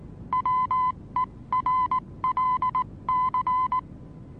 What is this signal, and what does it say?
Morse code, heard as a keyed tone, about 19 wpm, spaced as for 16 wpm, 1010 Hz. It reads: WERLC
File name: repos / Morsekurs / DJ4UF Morsen26